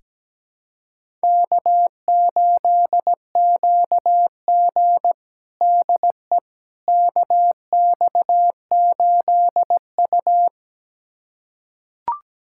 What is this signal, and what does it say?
Morse code, heard as a keyed tone, about 17 wpm, 700 Hz. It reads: K8QG DE KX8U
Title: USA Callsign DE Callsign - 4 Characters - 17wpm